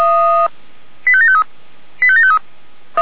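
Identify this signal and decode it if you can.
Five-tone selective call (ZVEI), heard as quick runs of five tones, tones of about 70 ms. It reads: 85753 85753
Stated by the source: Five-tone ZVEI selective call received on an SDR.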